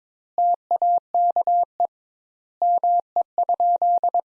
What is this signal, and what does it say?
Morse code, heard as a keyed tone, about 22 wpm, 700 Hz. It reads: TAKE ME?
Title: Sentences Top 200 Words - 22wpm